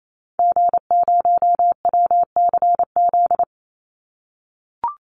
Morse code, heard as a keyed tone, about 28 wpm, 700 Hz. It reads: G0WCZ
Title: Call Signs - 28wpm